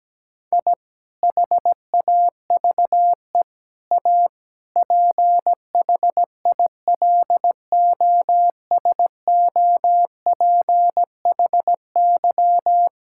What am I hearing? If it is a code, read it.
Morse code, heard as a keyed tone, about 17 wpm, 700 Hz. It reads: I HAVE A PHILOSOPHY
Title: Sentences from Top 1000 Words - Review Part 6 - 17wpm